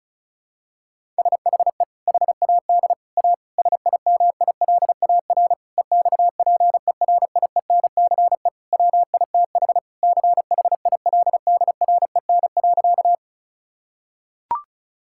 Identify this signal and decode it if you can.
Morse code, heard as a keyed tone, about 36 wpm, 700 Hz. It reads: SHE HAD A SIMILAR EXPERIENCE WITH CHILDREN.
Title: Sentences from Top 1000 Words - Review Part 5 - 35wpm